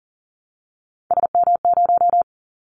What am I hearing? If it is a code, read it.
Morse code, heard as a keyed tone, about 38 wpm, 700 Hz. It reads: SM0
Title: Callsign Prefixes for Top 25 DXCC - 40wpm